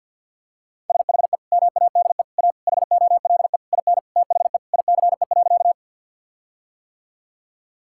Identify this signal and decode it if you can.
Morse code, heard as a keyed tone, about 52 wpm, 700 Hz. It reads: SHE MADE A HOLE IN THE ICE.
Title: Sentences from Top 1600 Words 50wpm